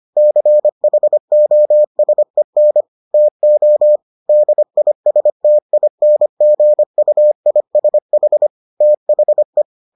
Morse code, heard as a keyed tone, about 25 wpm, 600 Hz. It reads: CHOSEN TO DISTINGUISH THE